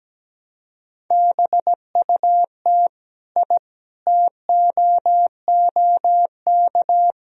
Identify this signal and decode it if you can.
Morse code, heard as a keyed tone, about 17 wpm, 700 Hz. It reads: BUT I TOOK
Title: Sentences from Top 1000 Words - Review Part 2 - 17wpm